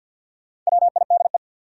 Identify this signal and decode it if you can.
Morse code, heard as a keyed tone, about 53 wpm, 700 Hz. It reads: WIDE
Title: Sentences from Top 900 Words - New Word Drill - 50wpm